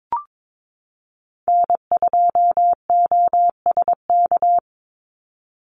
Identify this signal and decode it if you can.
Morse code, heard as a keyed tone, about 22 wpm, 700 Hz. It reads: N2OSK
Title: USA Call Signs 22wpm